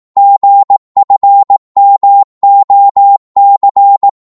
Morse code, heard as a keyed tone, about 18 wpm, 800 Hz. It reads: GFMOC
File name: Morse / ltr-18wpm